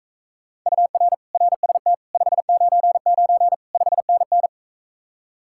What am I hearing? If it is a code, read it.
Morse code, heard as a keyed tone, about 45 wpm, 700 Hz. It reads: UR RST 599 5NN